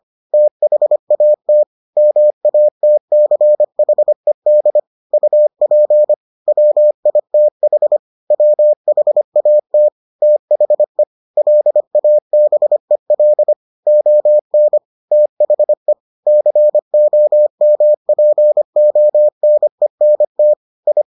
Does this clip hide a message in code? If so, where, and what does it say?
Yes: Morse code from 0.3 to 21.0 s, heard as a keyed tone, about 25 wpm, 600 Hz. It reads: THAT MATCHED UP WITH WHAT THE LABEL ON THE COMPONENT I